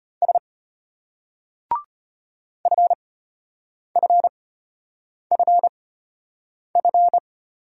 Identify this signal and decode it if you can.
Morse code, heard as a keyed tone, about 32 wpm, spaced as for 8 wpm, 700 Hz. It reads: S FFFF